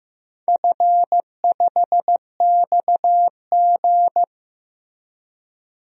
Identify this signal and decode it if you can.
Morse code, heard as a keyed tone, about 15 wpm, 700 Hz. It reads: F5XG
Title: DXCC-100 Call Signs - 4 Characters - Repeated 3x - 15wpm